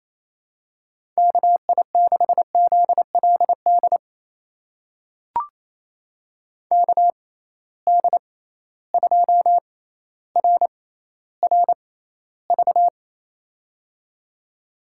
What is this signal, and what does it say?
Morse code, heard as a keyed tone, about 28 wpm, spaced as for 14 wpm, 700 Hz. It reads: KI6ZLD KD2RRV
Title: USA Call Signs - 6x Character Spacing - 28wpm